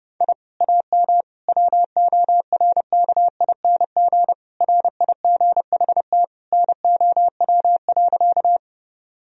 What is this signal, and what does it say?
Morse code, heard as a keyed tone, about 30 wpm, 700 Hz. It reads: I AM WORKING RIGHT NOW.
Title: Sentences from Top 1000 Words - Review Part 4 - 30wpm